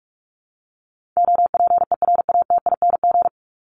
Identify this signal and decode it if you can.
Morse code, heard as a keyed tone, about 44 wpm, 700 Hz. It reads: OPERATING
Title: ING Suffix - Top 100 Words with Base Words 45wpm